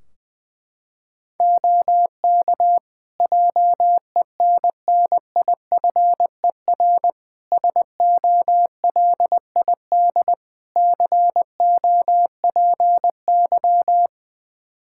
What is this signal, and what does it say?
Morse code, heard as a keyed tone, about 20 wpm, 700 Hz. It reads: OK JENNIFER SOLID COPY
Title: Practice QSO - Exchange 3 - More Info - 20wpm